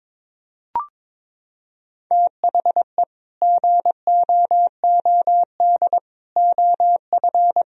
Morse code, heard as a keyed tone, about 22 wpm, 700 Hz. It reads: THE GOOD OF
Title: Sentences from Top 1000 Words - Review Part 4 - 22wpm